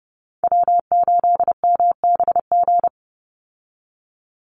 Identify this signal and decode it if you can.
Morse code, heard as a keyed tone, about 29 wpm, 700 Hz. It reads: W8MBG
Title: USA Call Signs 30wpm